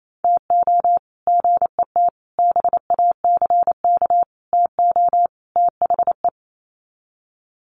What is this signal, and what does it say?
Morse code, heard as a keyed tone, about 28 wpm, 700 Hz. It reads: TO GET BACK TO THE
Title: Sets of 5 Words Encore 28wpm